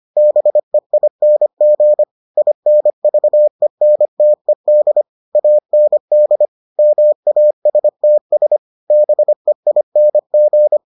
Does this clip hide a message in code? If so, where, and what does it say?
Yes: Morse code from 0.2 to 10.8 s, heard as a keyed tone, about 25 wpm, 600 Hz. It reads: BEING INVENTED AND MASTS BEING